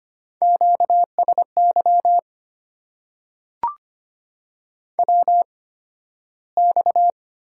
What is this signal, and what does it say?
Morse code, heard as a keyed tone, about 25 wpm, spaced as for 13 wpm, 700 Hz. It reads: QSY WX
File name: Top 90 QSO Elements - 8x Character Spacing - 25wpm